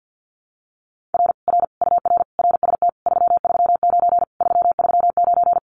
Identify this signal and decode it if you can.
Morse code, heard as a keyed tone, about 50 wpm, 700 Hz. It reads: R R UR RST 339 339
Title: Common QSO Phrases Encore 50wpm